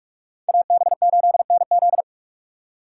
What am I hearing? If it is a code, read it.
Morse code, heard as a keyed tone, about 46 wpm, 700 Hz. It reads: AB8NZ
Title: USA Call Signs - 5 Characters - 3x Character Spacing - 45wpm